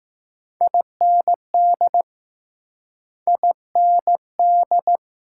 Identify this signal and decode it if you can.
Morse code, heard as a keyed tone, about 20 wpm, 700 Hz. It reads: IND IND